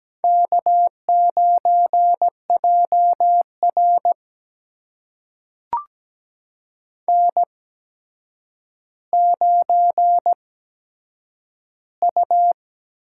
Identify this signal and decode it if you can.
Morse code, heard as a keyed tone, about 17 wpm, spaced as for 8 wpm, 700 Hz. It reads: K9JR N9U
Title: USA Call Signs - 4 Characters - 8x Character Spacing - 17wpm